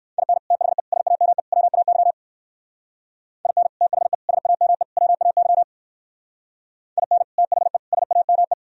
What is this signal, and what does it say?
Morse code, heard as a keyed tone, about 58 wpm, 700 Hz. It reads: IN THE SAME WAY IN THE SAME WAY IN THE SAME